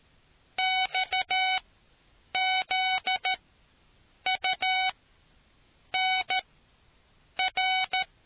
Morse code, heard as a keyed tone, about 13 wpm, spaced as for 7 wpm, 750 Hz. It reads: XZUNR